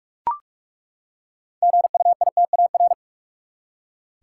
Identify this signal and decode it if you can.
Morse code, heard as a keyed tone, about 41 wpm, 700 Hz. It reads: GUITAR
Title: Six Letter Words - 45wpm